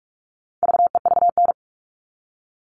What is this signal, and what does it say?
Morse code, heard as a keyed tone, about 44 wpm, 700 Hz. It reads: VEVN
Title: All Cut Numbers - 4-Digit 45wpm